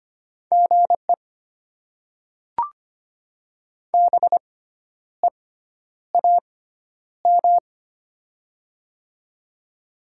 Morse code, heard as a keyed tone, about 25 wpm, spaced as for 10 wpm, 690 Hz. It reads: GE BEAM
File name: Top 90 QSO Elements - 6x Character Spacing - 25wpm